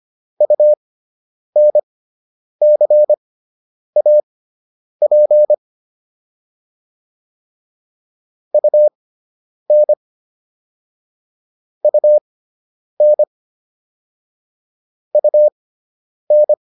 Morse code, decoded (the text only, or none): UNCAP UN UN UN